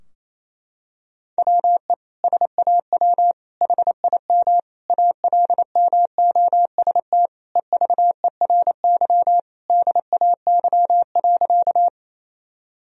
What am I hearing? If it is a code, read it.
Morse code, heard as a keyed tone, about 28 wpm, 700 Hz. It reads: WE SAW HIM ALMOST EVERY DAY.
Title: Sentences from Top 1000 Words - Review Part 4 - 28wpm